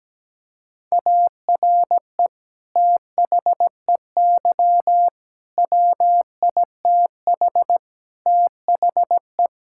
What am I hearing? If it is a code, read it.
Morse code, heard as a keyed tone, about 17 wpm, 700 Hz. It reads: ARE THEY WITH THE
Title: Sentences from Top 1000 Words - Review Part 5 - 17wpm